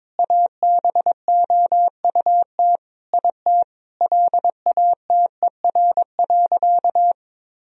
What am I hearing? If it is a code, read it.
Morse code, heard as a keyed tone, about 22 wpm, 700 Hz. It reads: ABOUT IT LATER.
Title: Sentences from Top 1000 Words - Review Part 4 - 22wpm